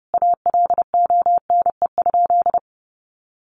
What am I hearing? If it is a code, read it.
Morse code, heard as a keyed tone, about 30 wpm, 700 Hz. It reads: ALONE?
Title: Sentences from Top 1100 Words 30wpm